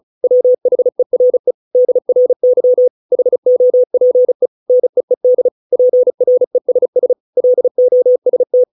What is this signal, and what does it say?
Morse code, heard as a keyed tone, about 35 wpm, 500 Hz. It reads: WHERE DRY HOPE NEED PRESS LOST